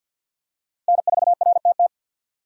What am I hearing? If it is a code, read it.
Morse code, heard as a keyed tone, about 50 wpm, 700 Hz. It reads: N4RTT
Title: USA Call Signs - 5 Characters - Repeated 2x - 50wpm